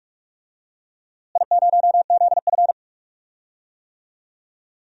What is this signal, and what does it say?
Morse code, heard as a keyed tone, about 41 wpm, 700 Hz. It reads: I0ZF
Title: DXCC-100 Call Signs - 4 Characters - Repeated 2x - 45wpm